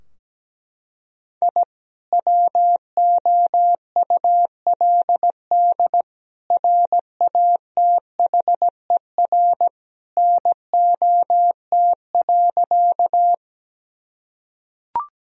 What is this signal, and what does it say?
Morse code, heard as a keyed tone, about 17 wpm, 700 Hz. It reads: I WOULD RATHER NOT.
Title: Sentences from Top 1000 Words - Review Part 4 - 17wpm